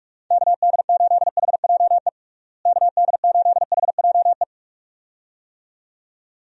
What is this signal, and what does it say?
Morse code, heard as a keyed tone, about 45 wpm, 700 Hz. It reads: KD8HJE KD8HJE
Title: USA Call Signs - 6 Characters - Repeated 2x - 45wpm